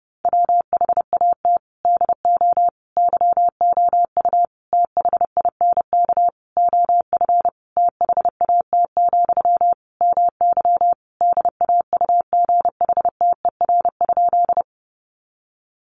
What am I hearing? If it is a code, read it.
Morse code, heard as a keyed tone, about 30 wpm, 700 Hz. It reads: WHAT DO YOU THINK OF THAT, MY DAUGHTER?
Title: Sentences from Top 1500 Words 30wpm